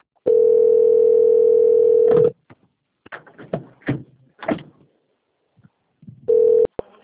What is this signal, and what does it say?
Ringback tone (North America), 440 + 480 Hz.